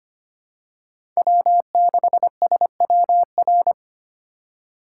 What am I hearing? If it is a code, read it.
Morse code, heard as a keyed tone, about 25 wpm, 700 Hz. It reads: W6SWR
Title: USA Call Signs - 5 Characters - 2x Character Spacing - 25wpm